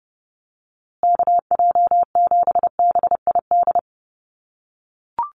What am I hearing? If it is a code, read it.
Morse code, heard as a keyed tone, about 30 wpm, 700 Hz. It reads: KJ7BID
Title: USA Call Signs Repeated 3x 30wpm